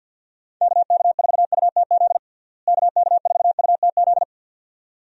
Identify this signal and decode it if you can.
Morse code, heard as a keyed tone, about 47 wpm, 700 Hz. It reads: KK4UTZ KK4UTZ
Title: USA Call Signs - 6 Characters - Repeated 2x - 50wpm